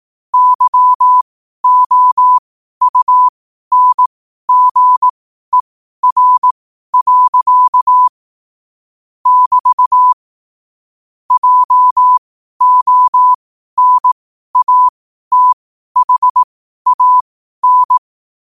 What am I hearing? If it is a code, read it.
Morse code, heard as a keyed tone, about 18 wpm, spaced as for 12 wpm, 1000 Hz. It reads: YOUNGER. <BT> JONATHAN
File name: qotdfiles / 12wpm